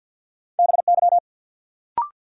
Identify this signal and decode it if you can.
Morse code, heard as a keyed tone, about 48 wpm, 700 Hz. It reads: BY
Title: Sentences from Top 100 Words - New Word Drill - 50wpm